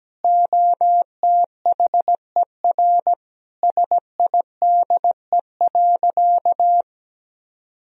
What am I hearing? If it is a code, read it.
Morse code, heard as a keyed tone, about 17 wpm, 700 Hz. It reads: OTHER SIDE.
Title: Sentences from Top 1000 Words - Review Part 1 - 17wpm